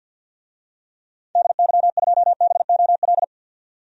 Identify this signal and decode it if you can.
Morse code, heard as a keyed tone, about 51 wpm, 700 Hz. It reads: DY2BOL
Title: DXCC-100 Call Signs - 6 Characters - 50wpm